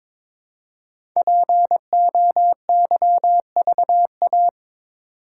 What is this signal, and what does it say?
Morse code, heard as a keyed tone, about 23 wpm, 700 Hz. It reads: POYVA
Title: Code Groups - 5 Letters - 22wpm